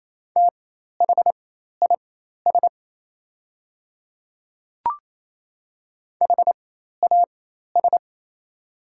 Morse code, heard as a keyed tone, about 28 wpm, spaced as for 13 wpm, 700 Hz. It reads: THIS HAS
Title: 4x Character Spacing - Top 100 Words - 28wpm